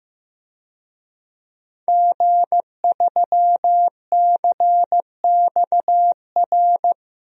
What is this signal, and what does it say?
Morse code, heard as a keyed tone, about 15 wpm, 700 Hz. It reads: G3CXR
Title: DXCC-100 Call Signs - 5 Characters - 15wpm